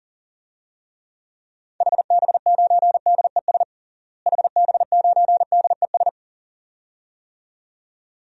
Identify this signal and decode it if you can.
Morse code, heard as a keyed tone, about 40 wpm, 700 Hz. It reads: HB9DES HB9DES
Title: DXCC-100 Call Signs - 6 Characters - Repeated 2x - 40wpm